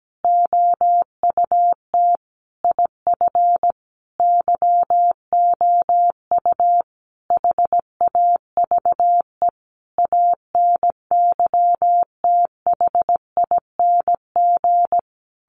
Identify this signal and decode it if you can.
Morse code, heard as a keyed tone, about 17 wpm, 700 Hz. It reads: OUT IF YOU HAVE ANYTHING
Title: Sentences from Top 1300 Words 17wpm